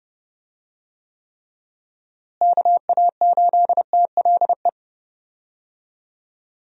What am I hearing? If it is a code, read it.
Morse code, heard as a keyed tone, about 30 wpm, 700 Hz. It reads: KA8TLE
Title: USA Call Signs - 6 Characters - Repeated 2x - 30wpm